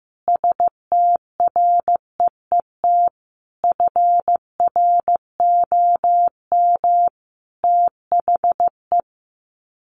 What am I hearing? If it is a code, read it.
Morse code, heard as a keyed tone, about 14 wpm, 700 Hz. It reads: STREET FROM THE
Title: Sets of 5 Words Encore 15wpm